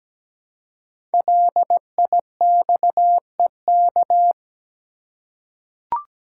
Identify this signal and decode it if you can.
Morse code, heard as a keyed tone, about 17 wpm, 700 Hz. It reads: LIXEK
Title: Code Groups - 5 Letter:Numbers - 17wpm